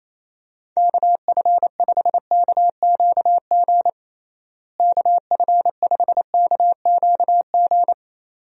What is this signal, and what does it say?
Morse code, heard as a keyed tone, about 28 wpm, 700 Hz. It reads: KF5KQG KF5KQG
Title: USA Call Signs - 6 Characters - Repeated 2x - 28wpm